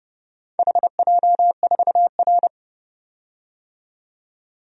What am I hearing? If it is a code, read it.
Morse code, heard as a keyed tone, about 32 wpm, 700 Hz. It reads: HJ4R
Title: DXCC-100 Call Signs - 4 Characters - 30wpm